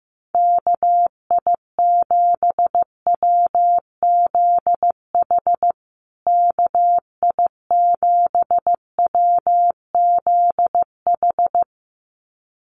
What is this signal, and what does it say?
Morse code, heard as a keyed tone, about 15 wpm, 700 Hz. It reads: KI7WZH KI7WZH
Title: USA Call Signs Repeated 2x 15wpm